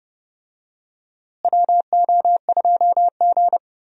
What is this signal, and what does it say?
Morse code, heard as a keyed tone, about 30 wpm, 700 Hz. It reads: WO2G
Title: USA Call Signs - 4 Characters - Repeated 2x - 30wpm